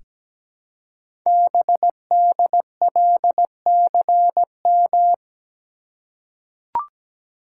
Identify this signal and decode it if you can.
Morse code, heard as a keyed tone, about 17 wpm, 700 Hz. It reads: BDLCM
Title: Code Groups - 5 Letters - 17wpm